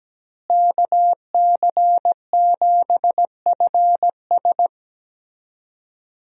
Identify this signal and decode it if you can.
Morse code, heard as a keyed tone, about 17 wpm, 700 Hz. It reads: KC7FS
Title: USA Call Signs - 5 Characters - Repeated 3x - 17wpm